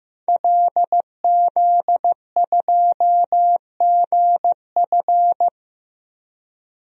Morse code, heard as a keyed tone, about 15 wpm, 700 Hz. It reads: LZ2GF
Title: DXCC-100 Call Signs - 5 Characters - 15wpm